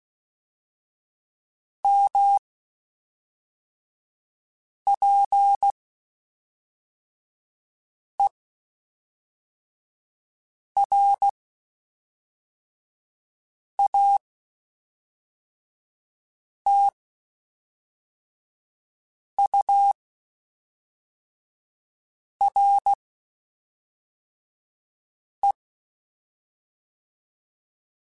Morse code, heard as a keyed tone, about 16 wpm, 800 Hz. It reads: M P E R A T U R E
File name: Spaced Characters / Spaced Characters-mp3s / Weather 4 21 sp @5